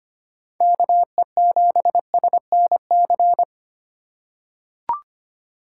Morse code, heard as a keyed tone, about 25 wpm, 700 Hz. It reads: KE7SNC
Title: USA Call Signs - 6 Characters - 3x Character Spacing - 25wpm